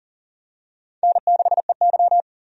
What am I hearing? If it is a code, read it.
Morse code, heard as a keyed tone, about 38 wpm, 700 Hz. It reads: N6EY
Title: USA Call Signs - 4 Characters - 40wpm